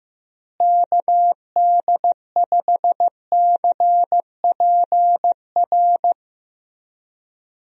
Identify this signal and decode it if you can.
Morse code, heard as a keyed tone, about 16 wpm, 700 Hz. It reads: KD5CPR
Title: USA Call Signs - 6 Characters - Repeated 2x - 15wpm